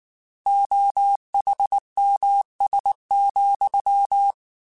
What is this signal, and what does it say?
Morse code, heard as a keyed tone, about 19 wpm, 800 Hz. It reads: OHMS,